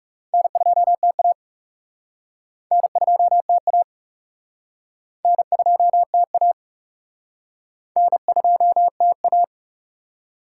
Morse code, heard as a keyed tone, about 41 wpm, 700 Hz. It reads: N2TA N2TA N2TA N2TA